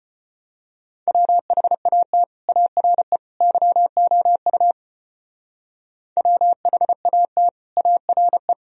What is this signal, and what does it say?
Morse code, heard as a keyed tone, about 31 wpm, 700 Hz. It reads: WHAT ARE YOU WHAT ARE
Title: Speed-Racing - Sets of 3 Words 25wpm